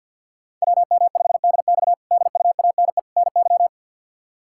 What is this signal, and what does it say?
Morse code, heard as a keyed tone, about 48 wpm, 700 Hz. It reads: WM5DX DUANE NY
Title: K1USN SST QSO Practice - 50wpm